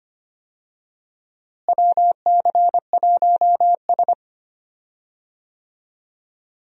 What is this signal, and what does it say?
Morse code, heard as a keyed tone, about 25 wpm, 700 Hz. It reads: WC1S
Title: USA Call Signs - 4 Characters - Repeated 3x - 25wpm